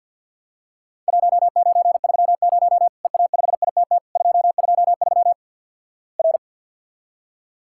Morse code, heard as a keyed tone, about 50 wpm, 700 Hz. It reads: 1930 EA5ITT 223 R
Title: Worked All Europe DX Contest - QTC Exchange with Repeats - 50wpm